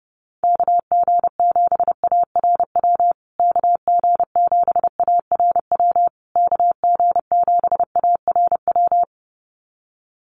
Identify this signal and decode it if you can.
Morse code, heard as a keyed tone, about 30 wpm, 700 Hz. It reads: KG7ARW KG7ARW KG7ARW
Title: USA Call Signs Repeated 3x 30wpm